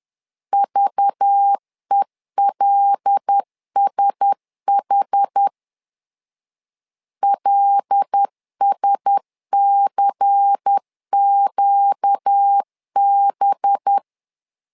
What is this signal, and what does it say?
Morse code, heard as a keyed tone, about 10 wpm, 800 Hz. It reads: VELSH LSCQB
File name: LEZIONI / 5 LBVC / 5 LBVC 10wpm 2349